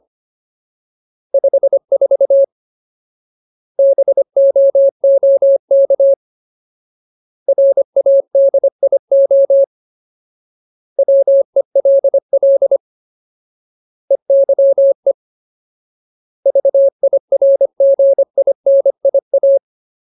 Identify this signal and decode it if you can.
Morse code, heard as a keyed tone, about 25 wpm, 550 Hz. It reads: 54 BOOK RADIO WELL EYE VIRGINIA
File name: Ultimate Word List / 25WPM Ultimate